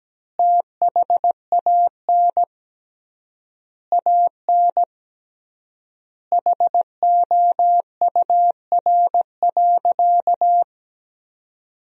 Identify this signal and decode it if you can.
Morse code, heard as a keyed tone, about 17 wpm, 700 Hz. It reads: THAN AN HOUR.
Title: 3x Word-Spacing - Sentences from Top 500 Words - 17wpm